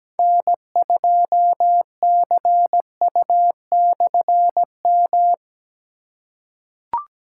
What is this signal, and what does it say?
Morse code, heard as a keyed tone, about 17 wpm, 700 Hz. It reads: N2CU/M
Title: K1USN SST QSO Practice - 17wpm